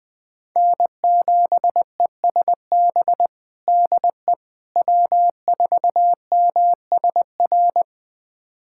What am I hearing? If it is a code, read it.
Morse code, heard as a keyed tone, about 20 wpm, 700 Hz. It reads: N7ESB DE W4MSR
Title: USA Callsign DE Callsign - 5 Characters - 20wpm